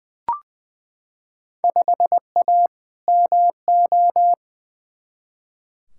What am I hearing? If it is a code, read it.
Morse code, heard as a keyed tone, about 20 wpm, 700 Hz. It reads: 5A MO
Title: Field Day Exchanges Encore - 20wpm